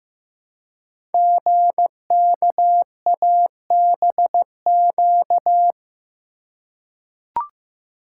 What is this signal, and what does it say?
Morse code, heard as a keyed tone, about 15 wpm, 700 Hz. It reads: GKABQ
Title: Code Groups - 5 Letters - 15wpm